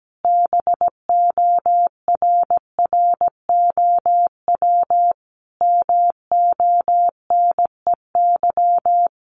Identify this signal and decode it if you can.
Morse code, heard as a keyed tone, about 17 wpm, 700 Hz. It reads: BORROW MONEY
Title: Sentences from Top 1500 Words 17wpm